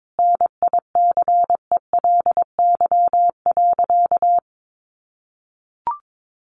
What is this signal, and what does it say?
Morse code, heard as a keyed tone, about 22 wpm, 700 Hz. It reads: NICELY.
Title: Easy Phrases 22wpm